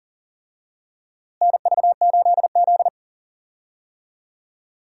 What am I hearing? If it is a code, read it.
Morse code, heard as a keyed tone, about 40 wpm, 700 Hz. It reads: NV8Z
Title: USA Call Signs - 4 Characters - Repeated 3x - 40wpm